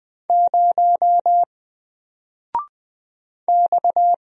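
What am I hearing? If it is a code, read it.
Morse code, heard as a keyed tone, about 20 wpm, 700 Hz. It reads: 0 X
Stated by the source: Single letter and number Morse drill at 20 wpm, from N4LCD.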